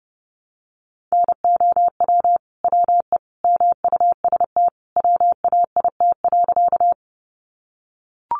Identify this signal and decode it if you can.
Morse code, heard as a keyed tone, about 30 wpm, 700 Hz. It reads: NOW WE MUST WAIT.
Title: Sentences from Top 500 Words 30wpm